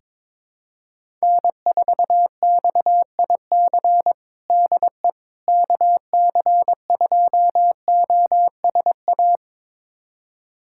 Morse code, heard as a keyed tone, about 21 wpm, 700 Hz. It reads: N4XIC DE KC2OSA
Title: USA Callsign DE Callsign 22wpm